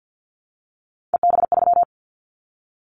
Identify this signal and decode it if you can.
Morse code, heard as a keyed tone, about 50 wpm, 700 Hz. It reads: EB3
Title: Callsign Prefixes for Top 25 DXCC - Repeated 3x - 50wpm